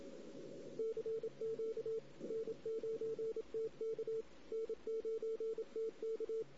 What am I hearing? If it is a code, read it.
Morse code, heard as a keyed tone, about 27 wpm, 440 Hz. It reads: CQ N9TK N9TK
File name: CW / 2008-ss-cw-k5tr-op-n5rz -16:00